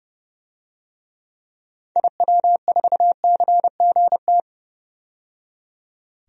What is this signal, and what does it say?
Morse code, heard as a keyed tone, about 32 wpm, 700 Hz. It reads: IW4CGT
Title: DXCC-100 Call Signs - 6 Characters - 30wpm